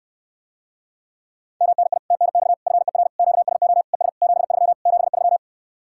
Morse code, heard as a keyed tone, about 74 wpm, 700 Hz. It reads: MNI TNX FER QSO ES 73 73